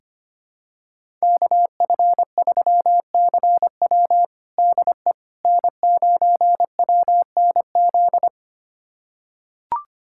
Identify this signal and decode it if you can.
Morse code, heard as a keyed tone, about 25 wpm, 700 Hz. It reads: KF3CW DE N9WNZ
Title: USA Callsign DE Callsign - 5 Characters - 25wpm